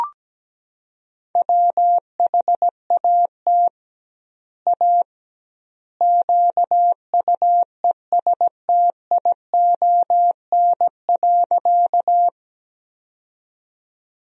Morse code, heard as a keyed tone, about 17 wpm, 700 Hz. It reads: WHAT A QUESTION.
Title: 2x Word-Spacing - Sentences from Top 400 Words - 17wpm